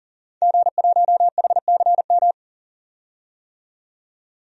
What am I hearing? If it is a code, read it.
Morse code, heard as a keyed tone, about 38 wpm, 700 Hz. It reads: G1HCM
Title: DXCC-100 Call Signs - 5 Characters - 40wpm